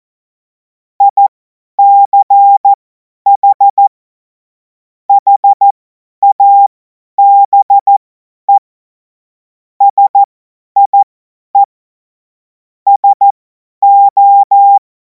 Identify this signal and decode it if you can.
Morse code, heard as a keyed tone, about 14 wpm, spaced as for 10 wpm, 800 Hz. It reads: ICH HABE SIE SO
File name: Morsekurs / Witze / Witz 04. August 2026 50 F70